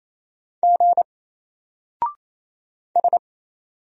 Morse code, heard as a keyed tone, about 28 wpm, 700 Hz. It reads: G S
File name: Single-Letter-Number-28wpm